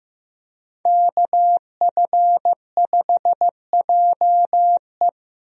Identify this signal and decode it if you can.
Morse code, heard as a keyed tone, about 15 wpm, 700 Hz. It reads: KF5JE
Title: USA Call Signs - 5 Characters - 15wpm